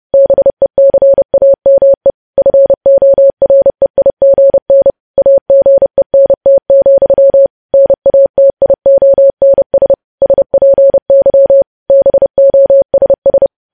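Morse code, heard as a keyed tone, about 30 wpm, 550 Hz. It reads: BECAME FOREIGN AGENT, NATIONS SPY BOSS
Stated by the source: Morse news headlines sent at 30 wpm.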